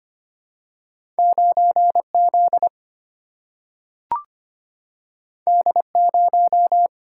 Morse code, heard as a keyed tone, about 25 wpm, 700 Hz. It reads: 9Z D0
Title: Single-Letter Single-Number 25wpm